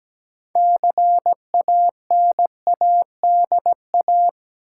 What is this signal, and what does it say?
Morse code, heard as a keyed tone, about 17 wpm, 700 Hz. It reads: CANADA
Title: Sets of 5 Words - New Word Drill - 17wpm